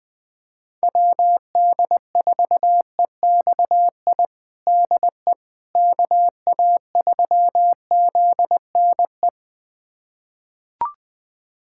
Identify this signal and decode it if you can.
Morse code, heard as a keyed tone, about 20 wpm, 700 Hz. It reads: WD4EXI DE KA3ZNE
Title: USA Callsign DE Callsign - 6 Characters - 20wpm